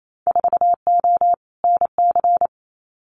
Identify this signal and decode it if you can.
Morse code, heard as a keyed tone, about 28 wpm, 700 Hz. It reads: 4O NC
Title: Winter Field Day Exchanges Encore 28wpm